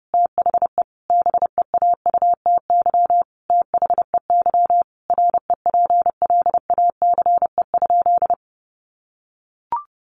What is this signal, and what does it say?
Morse code, heard as a keyed tone, about 30 wpm, 700 Hz. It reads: THE BEAUTY THEY REPLACE?
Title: Sentences from Top 1400 Words 30wpm